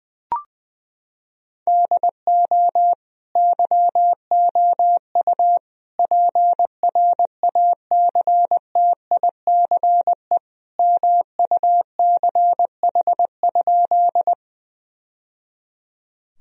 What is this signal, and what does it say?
Morse code, heard as a keyed tone, about 21 wpm, 700 Hz. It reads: DO YOU PRACTICE MUCH?
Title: Sentences from Top 1000 Words - Review Part 4 - 20wpm